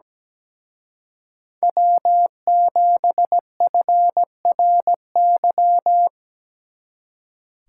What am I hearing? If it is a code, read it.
Morse code, heard as a keyed tone, about 17 wpm, 700 Hz. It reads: W7FRY
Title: Speed Racing - USA 1x3 Call Signs 17wpm